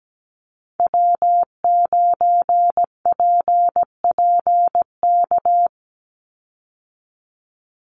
Morse code, heard as a keyed tone, about 17 wpm, 700 Hz. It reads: W9PPK
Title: USA Call Signs 17wpm